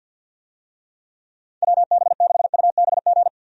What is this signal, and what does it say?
Morse code, heard as a keyed tone, about 53 wpm, 700 Hz. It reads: WB6UBG
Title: USA Call Signs - 6 Characters - 50wpm